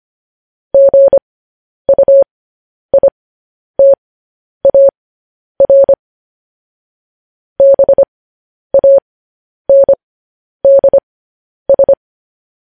Morse code, heard as a keyed tone, about 25 wpm, spaced as for 10 wpm, 550 Hz. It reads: GUITAR BANDS
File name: podcast / News headlines-1x-25x10